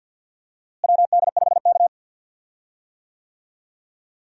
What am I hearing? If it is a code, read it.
Morse code, heard as a keyed tone, about 46 wpm, 700 Hz. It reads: WD5K
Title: USA Call Signs - 4 Characters - Repeated 2x - 50wpm